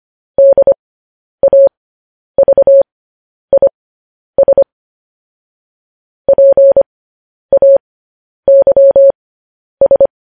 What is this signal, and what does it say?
Morse code, heard as a keyed tone, about 25 wpm, spaced as for 10 wpm, 550 Hz. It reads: DAVIS PAYS